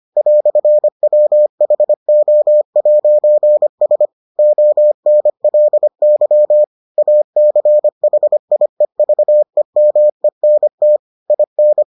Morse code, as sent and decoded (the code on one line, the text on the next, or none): .-..-. .-- .... --- .----. ... / --- -. .-.. -.-- / .- -.-. .... .. ...- . -- . -. - / .. -.
"WHO'S ONLY ACHIEVEMENT IN